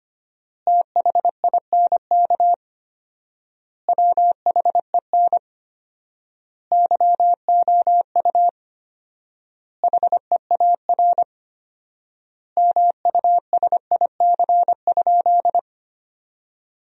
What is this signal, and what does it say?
Morse code, heard as a keyed tone, about 25 wpm, 700 Hz. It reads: THINK WHEN YOU HEAR MUSIC?
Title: 4x Word-Spacing - Sentences from Top 300 Words - 25wpm